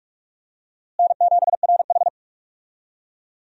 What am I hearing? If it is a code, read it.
Morse code, heard as a keyed tone, about 46 wpm, 700 Hz. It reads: N7RH
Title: USA Call Signs - 4 Characters - Repeated 3x - 45wpm